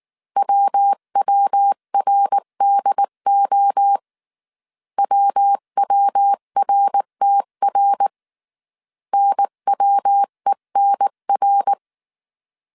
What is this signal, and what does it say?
Morse code, heard as a keyed tone, about 19 wpm, 800 Hz. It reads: WWRDO WWRTR NWENR
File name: CW / DET / 1 3 1500C S B 18wpm 1916